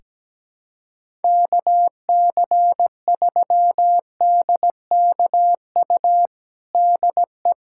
Morse code, heard as a keyed tone, about 17 wpm, 700 Hz. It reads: KC3DKU DE
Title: USA Callsign DE Callsign 17wpm